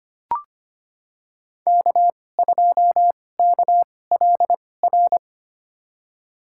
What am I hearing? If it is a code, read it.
Morse code, heard as a keyed tone, about 25 wpm, spaced as for 18 wpm, 700 Hz. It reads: K2KLR
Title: USA Call Signs - 2x Character Spacing - 25wpm